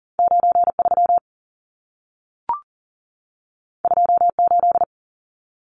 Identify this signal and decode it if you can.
Morse code, heard as a keyed tone, about 40 wpm, 700 Hz. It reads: 93 28
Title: Numbers - 2-Digit 40wpm